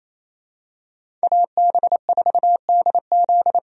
Morse code, heard as a keyed tone, about 28 wpm, 700 Hz. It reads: AB4DZ
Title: USA Call Signs - 5 Characters - 28wpm